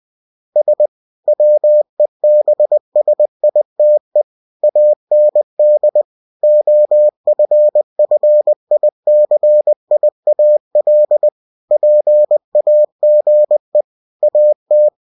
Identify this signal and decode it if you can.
Morse code, heard as a keyed tone, about 20 wpm, 600 Hz. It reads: S WEBSITE AND OFFICIAL PAGE AT